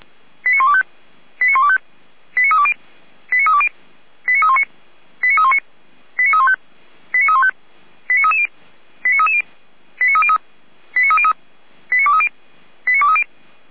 Five-tone selective call (ZVEI), heard as quick runs of five tones, tones of about 80 ms. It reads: 89125 89125 89320 89320 89319 89319 89315 89315 89330 89330 89303 89303 89230 89230
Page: SDR/ZVEI